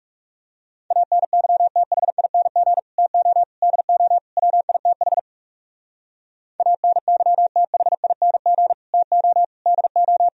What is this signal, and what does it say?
Morse code, heard as a keyed tone, about 34 wpm, 700 Hz. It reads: ANYTHING TO DO WITH ANYTHING TO DO
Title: Speed-Racing - Sets of 4 Words 30wpm